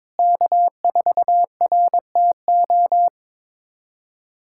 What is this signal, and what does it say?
Morse code, heard as a keyed tone, about 22 wpm, 700 Hz. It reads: K4RTO
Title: USA Callsign DE Callsign - 5 Characters - 22wpm